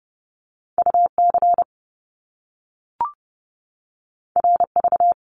Morse code, heard as a keyed tone, about 30 wpm, 700 Hz. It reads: UC RV
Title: Top Two Letter Combinations in Words 30wpm